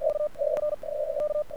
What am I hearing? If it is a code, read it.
Morse code, heard as a keyed tone, about 48 wpm, 1200 Hz. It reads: U R K
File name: archived / CRSnet